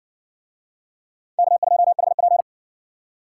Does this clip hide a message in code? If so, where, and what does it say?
Yes: Morse code from 1.4 to 2.4 s, heard as a keyed tone, about 58 wpm, 700 Hz. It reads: K2HP